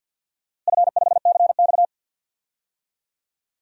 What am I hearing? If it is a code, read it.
Morse code, heard as a keyed tone, about 53 wpm, 700 Hz. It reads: F5CX
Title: DXCC-100 Call Signs - 4 Characters - Repeated 2x - 50wpm